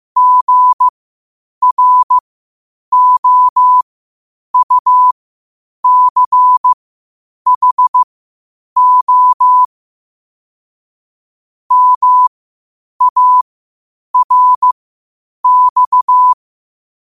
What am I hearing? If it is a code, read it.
Morse code, heard as a keyed tone, about 15 wpm, spaced as for 8 wpm, 1000 Hz. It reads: GROUCHO MARX
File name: qotdfiles / 8wpm